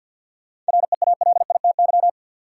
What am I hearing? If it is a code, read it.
Morse code, heard as a keyed tone, about 51 wpm, 700 Hz. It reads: REALITY
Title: Sentences from Top 1200 Words - New Word Drill Alt - 50wpm